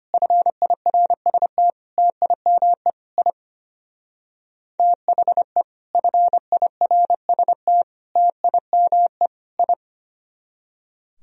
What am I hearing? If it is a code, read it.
Morse code, heard as a keyed tone, about 33 wpm, 700 Hz. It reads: FIRST TIME I THE FIRST TIME I